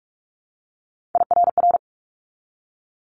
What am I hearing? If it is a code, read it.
Morse code, heard as a keyed tone, about 46 wpm, 700 Hz. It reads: IRR
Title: Multi-letter Phonemes 45wpm